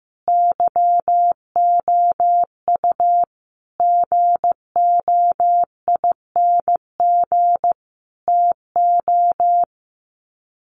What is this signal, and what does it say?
Morse code, heard as a keyed tone, about 15 wpm, 700 Hz. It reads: YOU GOING TO
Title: Sets of 5 Words Encore2 15wpm